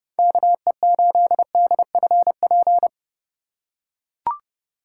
Morse code, heard as a keyed tone, about 30 wpm, 700 Hz. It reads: KE8DFP
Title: USA Call Signs - 6 Characters - 2x Character Spacing - 30wpm